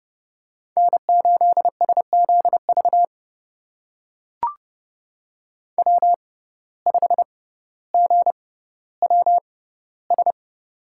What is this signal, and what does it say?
Morse code, heard as a keyed tone, about 30 wpm, spaced as for 15 wpm, 700 Hz. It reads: N8SZV W5GWS
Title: USA Call Signs - 5 Characters - 6x Character Spacing - 30wpm